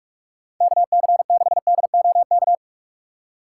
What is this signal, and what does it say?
Morse code, heard as a keyed tone, about 46 wpm, 700 Hz. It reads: KC6DOK
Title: USA Call Signs - 6 Characters - 4x Character Spacing - 45wpm